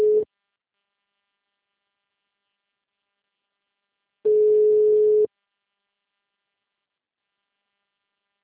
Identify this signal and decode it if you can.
Ringback tone (Europe, 425 Hz).